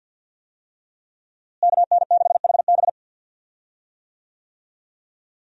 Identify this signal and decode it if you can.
Morse code, heard as a keyed tone, about 52 wpm, 700 Hz. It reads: KN6HB
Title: USA Call Signs - 5 Characters - Repeated 3x - 50wpm